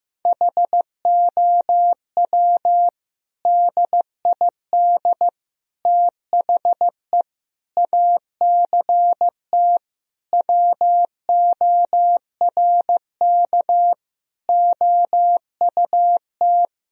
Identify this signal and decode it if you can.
Morse code, heard as a keyed tone, about 15 wpm, 700 Hz. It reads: HOW DID THE ACT WORK OUT
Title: Sentences from Top 1000 Words - Review Part 4 - 15wpm